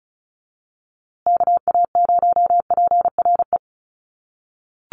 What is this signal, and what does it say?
Morse code, heard as a keyed tone, about 34 wpm, 700 Hz. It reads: KA0PRE
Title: USA Call Signs Repeated 3x 35wpm